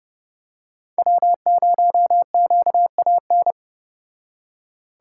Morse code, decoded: W0QAN

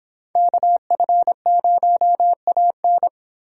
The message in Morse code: KF0AN

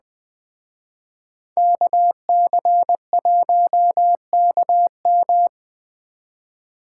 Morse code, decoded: KC1KM

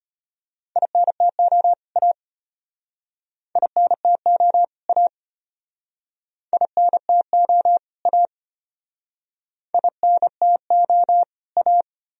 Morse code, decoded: INTO A INTO A INTO A INTO A

